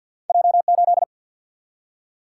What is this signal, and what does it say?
Morse code, heard as a keyed tone, about 53 wpm, 700 Hz. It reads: J8
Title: Single-Letter Single-Number 50wpm